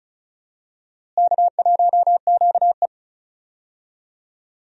Morse code, heard as a keyed tone, about 36 wpm, 700 Hz. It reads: K1QE
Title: USA Call Signs - 4 Characters - Repeated 2x - 35wpm